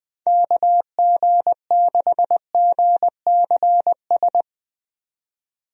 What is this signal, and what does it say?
Morse code, heard as a keyed tone, about 21 wpm, 700 Hz. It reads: KG6GCS
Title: USA Call Signs - 6 Characters - 20wpm